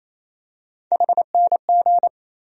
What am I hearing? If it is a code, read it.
Morse code, heard as a keyed tone, about 28 wpm, 700 Hz. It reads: HNG
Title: USA Call Sign Postfixes - 3 Letters 28wpm